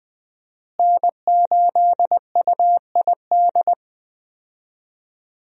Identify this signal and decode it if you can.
Morse code, heard as a keyed tone, about 20 wpm, 700 Hz. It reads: N8UID
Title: USA Call Signs - 5 Characters - Repeated 2x - 20wpm